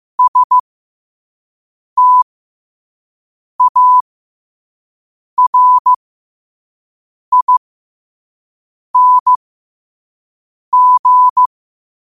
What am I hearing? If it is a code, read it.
Morse code, heard as a keyed tone, about 15 wpm, spaced as for 5 wpm, 1000 Hz. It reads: STARING
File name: qotdfiles / 5wpm